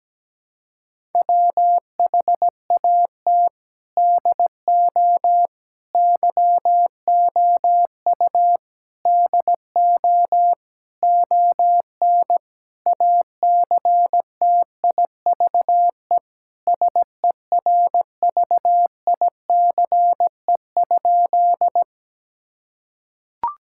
Morse code, heard as a keyed tone, about 17 wpm, 700 Hz. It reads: WHAT DO YOU DO ON ACTIVE SERVICE?
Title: Sentences from Top 1000 Words - Review Part 6 - 17wpm